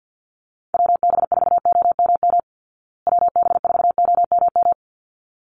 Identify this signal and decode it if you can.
Morse code, heard as a keyed tone, about 50 wpm, 700 Hz. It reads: WB4OMM WB4OMM
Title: Call Signs Repeated 3x - 50wpm